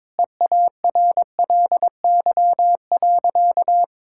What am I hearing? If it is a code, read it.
Morse code, heard as a keyed tone, about 22 wpm, 700 Hz. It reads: EARLY.
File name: practice / 22wpm / Sentences with 3-Word Phrases - 22wpm